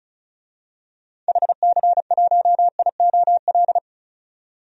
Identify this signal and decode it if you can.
Morse code, heard as a keyed tone, about 36 wpm, 690 Hz. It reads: HC1IOL
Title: DXCC-100 Call Signs - 6 Characters - 35wpm